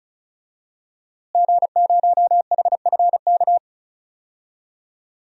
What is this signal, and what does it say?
Morse code, heard as a keyed tone, about 36 wpm, 700 Hz. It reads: G0HFK
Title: DXCC-100 Call Signs - 5 Characters - Repeated 3x - 35wpm